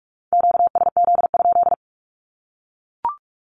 Morse code, heard as a keyed tone, about 44 wpm, 700 Hz. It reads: QSZ?
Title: Q-Codes Repeated 3x 45wpm